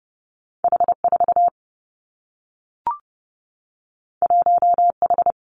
Morse code, heard as a keyed tone, about 30 wpm, 700 Hz. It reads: H 4 1 H
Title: Two Challenging Letter-Number Combinations 30wpm